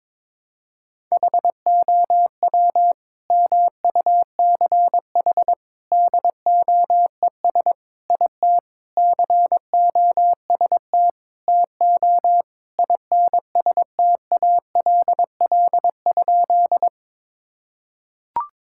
Morse code, heard as a keyed tone, about 22 wpm, 700 Hz. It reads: HOW MUCH DOES IT COST TO INSTALL?
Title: Sentences from Top 1600 Words 22wpm